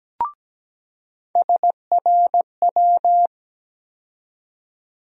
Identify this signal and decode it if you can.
Morse code, heard as a keyed tone, about 17 wpm, 700 Hz. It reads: SRW
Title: USA Call Sign Postfixes - 3 Letters 17wpm